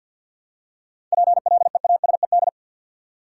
Morse code, heard as a keyed tone, about 53 wpm, 700 Hz. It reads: PLEASED
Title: Seven Letter Words - 50wpm